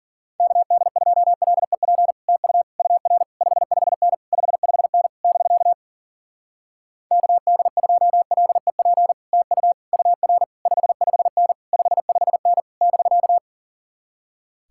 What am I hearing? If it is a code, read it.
Morse code, heard as a keyed tone, about 50 wpm, 700 Hz. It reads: KD2LEP TU UR 55N 55N <BK> KD2LEP TU UR 55N 55N <BK>
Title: Speed-Racing Common QSO Phrases Encore - 40wpm